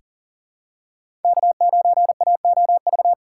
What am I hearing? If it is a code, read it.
Morse code, heard as a keyed tone, about 40 wpm, 700 Hz. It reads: K9AOV
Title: USA Call Signs - 5 Characters - 2x Character Spacing - 40wpm